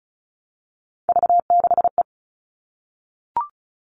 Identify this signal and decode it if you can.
Morse code, heard as a keyed tone, about 34 wpm, 700 Hz. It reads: V6E
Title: All Cut Numbers - 3-Digit 35wpm